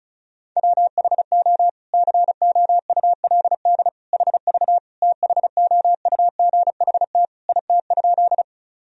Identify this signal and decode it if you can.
Morse code, heard as a keyed tone, about 34 wpm, 700 Hz. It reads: WHO COULD HV THOUGHT IT?
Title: Sentences from Top 300 Words and Abbreviations 35wpm